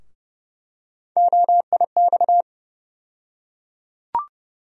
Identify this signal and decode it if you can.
Morse code, heard as a keyed tone, about 30 wpm, 700 Hz. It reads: OIX
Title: USA Call Sign Postfixes - 3 Letters 30wpm